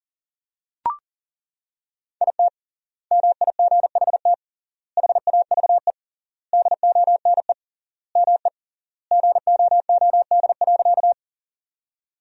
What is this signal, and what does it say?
Morse code, heard as a keyed tone, about 40 wpm, 700 Hz. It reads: IT MIGHT HAVE DONE ME GOOD.
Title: 3x Word-Spacing - Sentences from Top 500 Words - 40wpm